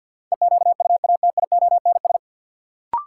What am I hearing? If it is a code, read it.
Morse code, heard as a keyed tone, about 52 wpm, 700 Hz. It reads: EQUATIONS
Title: Sets of 5 Words - New Word Drill - 50wpm